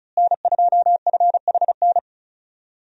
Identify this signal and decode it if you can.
Morse code, heard as a keyed tone, about 36 wpm, 700 Hz. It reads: N2FHN
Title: USA Call Signs - 5 Characters - Repeated 2x - 35wpm